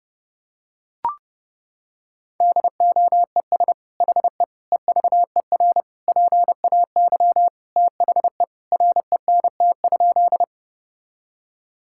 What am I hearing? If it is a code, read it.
Morse code, heard as a keyed tone, about 30 wpm, 700 Hz. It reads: DOES HE EVER PAY THE RENT?
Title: Sentences from Top 1000 Words - Review Part 5 - 30wpm